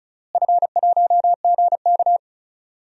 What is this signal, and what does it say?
Morse code, heard as a keyed tone, about 36 wpm, 700 Hz. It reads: F1GK
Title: DXCC-100 Call Signs - 4 Characters - Repeated 3x - 35wpm